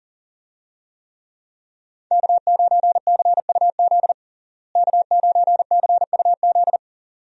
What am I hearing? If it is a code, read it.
Morse code, heard as a keyed tone, about 37 wpm, 700 Hz. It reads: K9CUZ K9CUZ
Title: USA Call Signs - 5 Characters - Repeated 2x - 40wpm